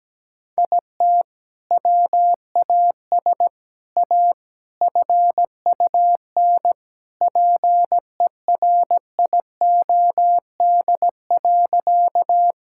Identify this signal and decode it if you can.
Morse code, heard as a keyed tone, about 17 wpm, 700 Hz. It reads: IT WAS A FUN PERIOD.
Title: Sentences from Top 1000 Words - Review Part 5 - 17wpm